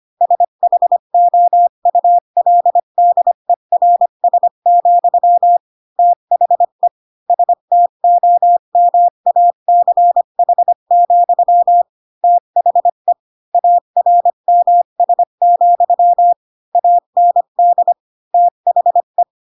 Morse code, heard as a keyed tone, about 25 wpm, 700 Hz. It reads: SHOULDERS, THE STOMACH, THE ARMS, AND THE
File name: Pinocchio / Pinocchio ch3 20 25 ik1hge